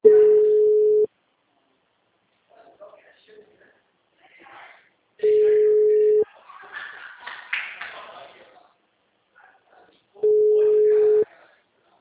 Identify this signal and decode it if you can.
Ringback tone (Europe, 425 Hz), with speech.
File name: out- -2038- - - .79455